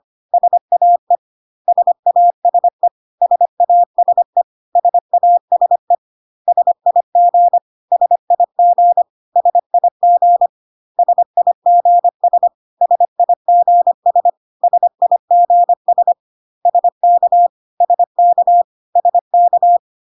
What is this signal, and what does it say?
Morse code, heard as a keyed tone, about 25 wpm, 700 Hz. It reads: SAE SASE SASE SASE SIG SIG SIG SIGS SIGS SIGS SK SK SK
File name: cw / CW common abbreviations alphabetical repeated 3 times 25 25wpm ik1hge